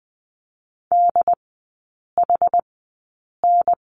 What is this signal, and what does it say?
Morse code, heard as a keyed tone, about 20 wpm, spaced as for 8 wpm, 700 Hz. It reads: DHN
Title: 067 Lesson 12e - Letter d - Three Characters 20wpm